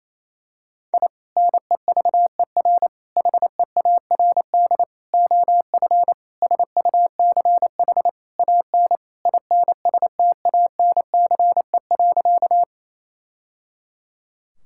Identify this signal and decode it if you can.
Morse code, heard as a keyed tone, about 27 wpm, 700 Hz. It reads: I NEVER HEARD OF SUCH AN INSTANCE.
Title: Sentences from Top 1000 Words - Review Part 5 - 28wpm